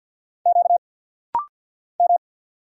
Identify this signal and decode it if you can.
Morse code, heard as a keyed tone, about 50 wpm, 690 Hz. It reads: Q M
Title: Single Letters - Rapid-Fire 50wpm